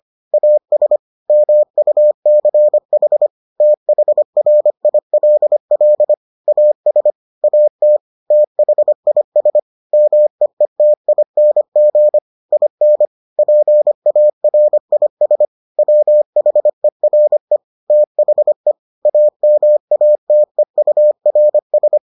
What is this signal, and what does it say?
Morse code, heard as a keyed tone, about 25 wpm, 600 Hz. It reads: AS MUCH THRILL AS AT THIS MEETING IN PARIS WHERE THE AMATEURS